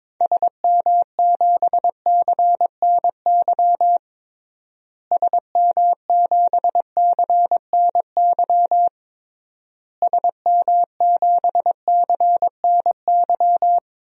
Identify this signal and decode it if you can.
Morse code, heard as a keyed tone, about 22 wpm, 700 Hz. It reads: SM7CNY SM7CNY SM7CNY
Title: DXCC-100 Call Signs - 6 Characters - Repeated 3x - 22wpm